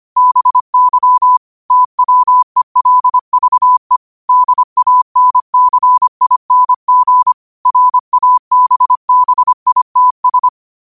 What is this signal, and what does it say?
Morse code, heard as a keyed tone, about 25 wpm, 1000 Hz. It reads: BY TWELVE DANCING RABBITS
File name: qotdfiles / 25wpm